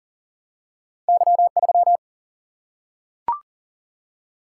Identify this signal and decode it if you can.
Morse code, heard as a keyed tone, about 40 wpm, 700 Hz. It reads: Y3
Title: Single-Letter Single-Number 40wpm